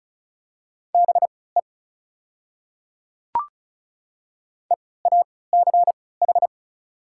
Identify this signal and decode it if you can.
Morse code, heard as a keyed tone, about 36 wpm, spaced as for 20 wpm, 690 Hz. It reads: BE EACH
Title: 3x Character Spacing - Top 100 Words - 35wpm